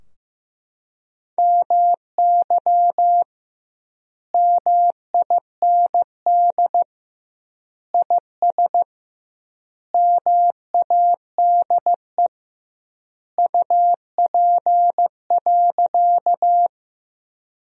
Morse code, heard as a keyed tone, about 15 wpm, 700 Hz. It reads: MY MIND IS MADE UP.
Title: 2x Word-Spacing - Sentences from Top 500 Words - 15wpm